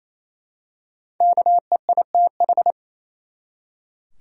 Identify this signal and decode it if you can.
Morse code, heard as a keyed tone, about 28 wpm, 700 Hz. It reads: KEITH
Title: USA Common Names - Max 5 Letters - 28wpm